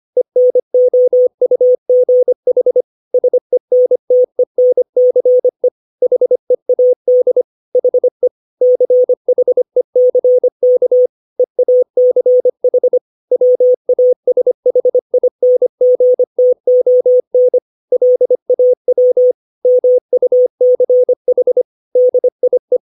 Morse code, heard as a keyed tone, about 25 wpm, 500 Hz. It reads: ENOUGH SENTENCE HEAD HE CHECK EACH WASHINGTON LAW MUCH DIE
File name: 632 words / 632 Words 25WPM